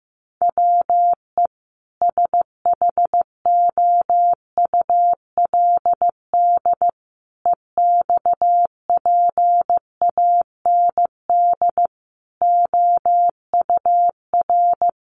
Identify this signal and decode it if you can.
Morse code, heard as a keyed tone, about 15 wpm, 700 Hz. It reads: WE SHOULD EXPAND OUR
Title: Sentences from Top 1400 Words 15wpm